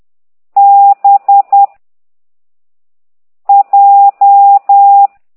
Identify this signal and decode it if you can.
Morse code, heard as a keyed tone, about 10 wpm, 800 Hz. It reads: B J
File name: ct2idw / exerc15